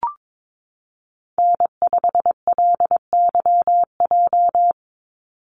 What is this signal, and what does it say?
Morse code, heard as a keyed tone, about 22 wpm, 700 Hz. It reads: N5LYJ
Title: Call Signs Repeated 3x - 22wpm